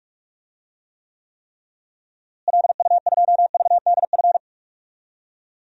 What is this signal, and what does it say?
Morse code, heard as a keyed tone, about 46 wpm, 700 Hz. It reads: LU2VDF
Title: DXCC-100 Call Signs - 6 Characters - Repeated 3x - 45wpm